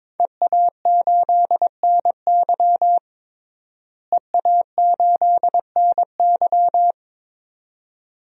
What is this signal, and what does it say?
Morse code, heard as a keyed tone, about 21 wpm, 700 Hz. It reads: EA8NY EA8NY
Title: DXCC-100 Call Signs - 5 Characters - Repeated 2x - 22wpm